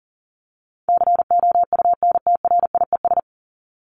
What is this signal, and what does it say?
Morse code, heard as a keyed tone, about 40 wpm, 700 Hz. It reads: COUNTRIES
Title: ES:S Suffix - Top 100 Words with Base Words 40wpm